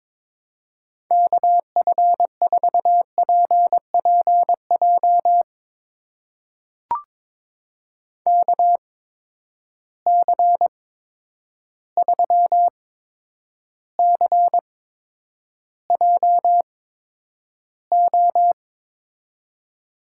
Morse code, heard as a keyed tone, about 22 wpm, spaced as for 9 wpm, 700 Hz. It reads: KF4PPJ KC3CJO
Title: USA Call Signs - 6 Characters - 8x Character Spacing - 22wpm